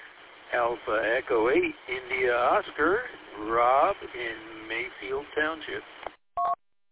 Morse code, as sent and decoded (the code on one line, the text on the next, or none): none